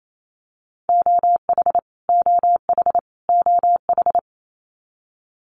Morse code, heard as a keyed tone, about 28 wpm, 700 Hz. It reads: OH OH OH
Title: Top 500 Words Repeated 28wpm